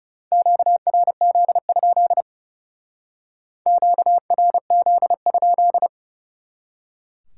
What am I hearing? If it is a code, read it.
Morse code, heard as a keyed tone, about 40 wpm, 700 Hz. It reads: QRZ? QRZ?